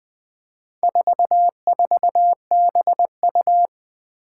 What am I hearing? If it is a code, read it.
Morse code, heard as a keyed tone, about 21 wpm, 700 Hz. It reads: 44BU
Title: Four Challenging Letter-Number Combinations 20wpm